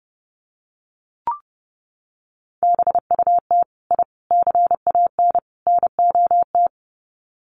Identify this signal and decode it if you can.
Morse code, heard as a keyed tone, about 30 wpm, 700 Hz. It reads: BUT I CAN NOT
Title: Sets of 4 Words 30wpm